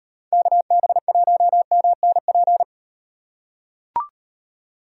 Morse code, heard as a keyed tone, about 38 wpm, 700 Hz. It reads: KB1MNP